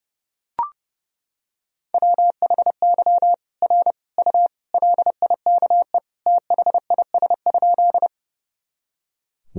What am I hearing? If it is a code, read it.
Morse code, heard as a keyed tone, about 32 wpm, 700 Hz. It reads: WHY R U LIKE THIS?
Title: Sentences from Top 200 Words and Abbreviations 30wpm